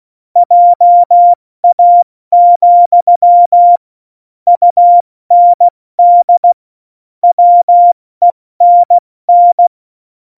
Morse code, heard as a keyed tone, about 16 wpm, 700 Hz. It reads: JA, UND WENN
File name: Morsekurs / Witze / Witz 22. August 17, Tempo 70 F80